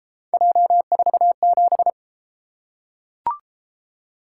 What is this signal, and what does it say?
Morse code, heard as a keyed tone, about 34 wpm, 700 Hz. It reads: J47